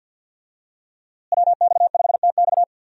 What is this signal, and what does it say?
Morse code, heard as a keyed tone, about 46 wpm, 700 Hz. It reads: WX5TX
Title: USA Call Signs - 5 Characters - Repeated 3x - 50wpm